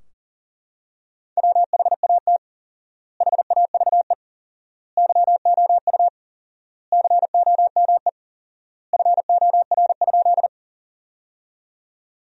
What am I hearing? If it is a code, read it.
Morse code, heard as a keyed tone, about 40 wpm, 700 Hz. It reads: WHAT HAVE YOU COME FOR?
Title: 4x Word-Spacing - Sentences from Top 100 Words - 40wpm